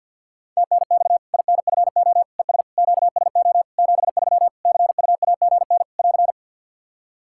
Morse code, heard as a keyed tone, about 50 wpm, 700 Hz. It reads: TNX INFO ES QSO 73 CUAGN <AR>
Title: K1USN SST QSO Practice 50wpm